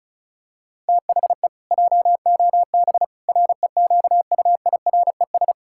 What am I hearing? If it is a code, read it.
Morse code, heard as a keyed tone, about 34 wpm, 700 Hz. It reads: THE JOB REQUIRES
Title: Pangram Challenge v2 - 35wpm